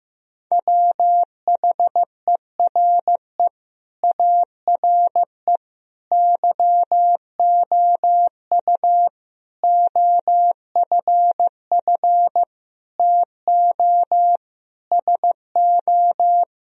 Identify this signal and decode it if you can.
Morse code, heard as a keyed tone, about 15 wpm, 700 Hz. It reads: WHERE ARE YOU OFF TO SO
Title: Sentences from Top 1000 Words - Review Part 2 - 15wpm